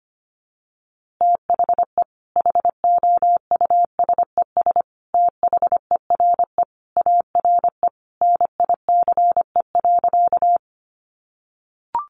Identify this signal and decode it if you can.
Morse code, heard as a keyed tone, about 25 wpm, 700 Hz. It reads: THE HOUSES THERE ARE NICE.
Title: Easy Phrases 25wpm